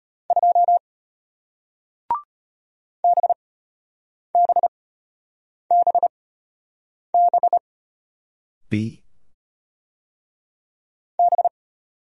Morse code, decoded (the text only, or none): 2 BBBB B